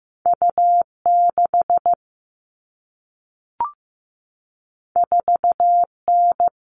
Morse code, heard as a keyed tone, about 15 wpm, 700 Hz. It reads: U6 4N
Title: All Cut Numbers - 2-Digit 15wpm